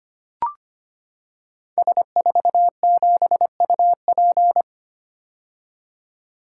Morse code, heard as a keyed tone, about 25 wpm, 700 Hz. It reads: S47UP
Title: Code Groups - 5 Letter:Numbers - 25wpm